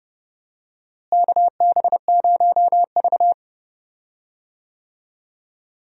Morse code, heard as a keyed tone, about 30 wpm, 700 Hz. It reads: KB0V
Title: USA Call Signs - 4 Characters - Repeated 3x - 30wpm